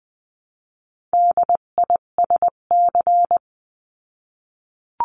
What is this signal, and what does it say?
Morse code, heard as a keyed tone, about 20 wpm, 700 Hz. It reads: DISC
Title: 258 Lesson 42c - Words 16-20wpm